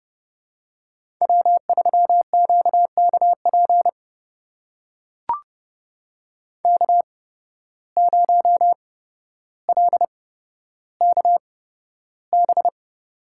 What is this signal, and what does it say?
Morse code, heard as a keyed tone, about 30 wpm, spaced as for 13 wpm, 700 Hz. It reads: W3QKP K0LKB
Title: USA Call Signs - 5 Characters - 8x Character Spacing - 30wpm